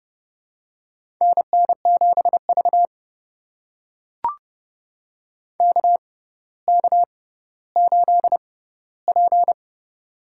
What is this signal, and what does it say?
Morse code, heard as a keyed tone, about 30 wpm, spaced as for 15 wpm, 700 Hz. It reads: NN7V KK8P
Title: USA Call Signs - 6x Character Spacing - 30wpm